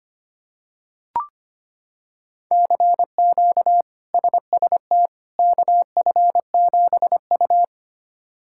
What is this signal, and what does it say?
Morse code, heard as a keyed tone, about 25 wpm, 700 Hz. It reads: CQ SST KF7U
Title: K1USN SST QSO Practice - 25wpm